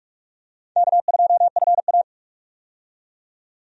Morse code, heard as a keyed tone, about 45 wpm, 700 Hz. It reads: K2FA
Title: USA Call Signs - 4 Characters - 45wpm